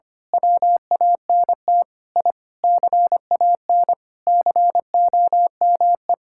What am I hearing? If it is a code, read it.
Morse code, heard as a keyed tone, about 25 wpm, 700 Hz. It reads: WANT I CAN COME
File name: practice / 25wpm / Sentences with 3-Word Phrases - 25wpm